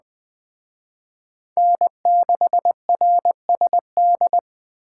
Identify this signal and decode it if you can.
Morse code, heard as a keyed tone, about 20 wpm, 700 Hz. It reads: N6RSD